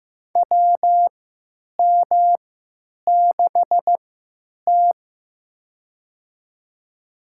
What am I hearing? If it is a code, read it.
Morse code, heard as a keyed tone, about 15 wpm, spaced as for 9 wpm, 700 Hz. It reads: WM6T
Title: USA Call Signs - 3x Character Spacing - 15wpm